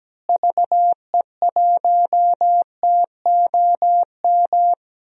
Morse code, heard as a keyed tone, about 17 wpm, 700 Hz. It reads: VE1TOM
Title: DXCC-100 Call Signs - 6 Characters - Repeated 3x - 17wpm